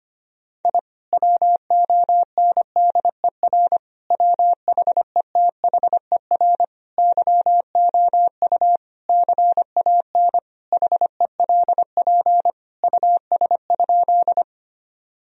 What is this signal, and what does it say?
Morse code, heard as a keyed tone, about 25 wpm, 700 Hz. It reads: I WONDER WHETHER YOU CAN HELP US?
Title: Sentences from Top 1000 Words - Review Part 3 - 25wpm